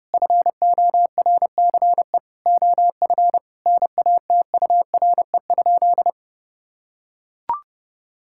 Morse code, heard as a keyed tone, about 30 wpm, 700 Hz. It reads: FORCE OF NATURE?
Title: Sentences from Top 1000 Words - Review Part 5 - 30wpm